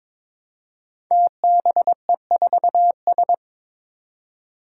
Morse code, heard as a keyed tone, about 22 wpm, 700 Hz. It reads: TBE4S
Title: Five Challenging Letter-Number Combinations 22wpm